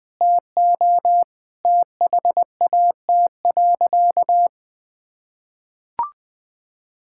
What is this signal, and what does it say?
Morse code, heard as a keyed tone, about 21 wpm, 700 Hz. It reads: TO THAT.
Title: Sentences with 3-Word Phrases - 20wpm